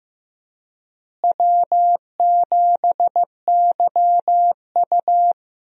Morse code, heard as a keyed tone, about 15 wpm, 700 Hz. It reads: W7YU
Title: USA Call Signs - 4 Characters - Repeated 3x - 15wpm